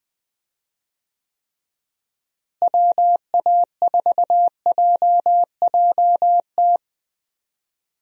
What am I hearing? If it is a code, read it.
Morse code, heard as a keyed tone, about 20 wpm, 700 Hz. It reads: WA4JJT
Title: USA Call Signs - 6 Characters - Repeated 2x - 20wpm